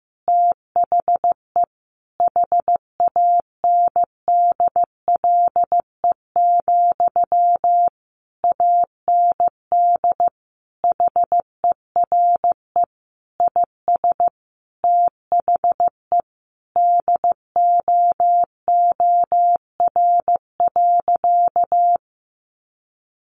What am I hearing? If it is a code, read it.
Morse code, heard as a keyed tone, about 15 wpm, 700 Hz. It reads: THE HANDLE, AND HERE IS THE DOOR.
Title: Sentences from Top 1100 Words 15wpm